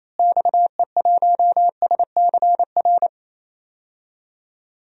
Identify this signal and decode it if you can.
Morse code, heard as a keyed tone, about 28 wpm, 700 Hz. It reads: XE1SCR
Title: DXCC-100 Call Signs - 6 Characters - Repeated 3x - 28wpm